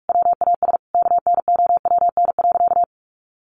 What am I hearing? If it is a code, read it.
Morse code, heard as a keyed tone, about 44 wpm, 700 Hz. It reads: WAS KNOWN.
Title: Sentences from Top 1400 Words 45wpm